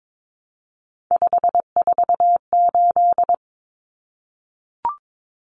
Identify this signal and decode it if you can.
Morse code, heard as a keyed tone, about 22 wpm, 700 Hz. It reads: 548
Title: Numbers - 3-Digit 22wpm